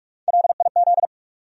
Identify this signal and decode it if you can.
Morse code, heard as a keyed tone, about 43 wpm, 700 Hz. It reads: LIZ
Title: USA Common Names - Max 3 Letters - 45wpm